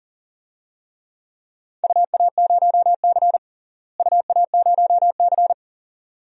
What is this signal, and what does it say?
Morse code, heard as a keyed tone, about 38 wpm, 700 Hz. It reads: UA0C UA0C
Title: DXCC-100 Call Signs - 4 Characters - Repeated 2x - 40wpm